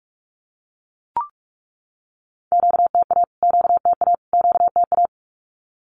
Morse code, heard as a keyed tone, about 44 wpm, 700 Hz. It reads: QTA QTA QTA
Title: Q-Codes Repeated 3x 45wpm